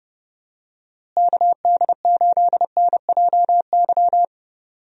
Morse code, heard as a keyed tone, about 30 wpm, 700 Hz. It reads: KD8NJY
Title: USA Call Signs - 6 Characters - Repeated 2x - 30wpm